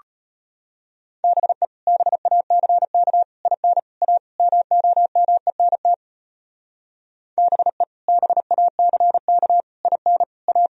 Morse code, heard as a keyed tone, about 30 wpm, 700 Hz. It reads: BE BACK IN A MOMENT BE BACK IN A
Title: Speed-Racing - Sets of 5 Words 25wpm